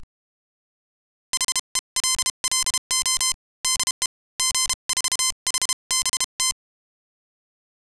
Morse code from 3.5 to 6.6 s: DE G4HBT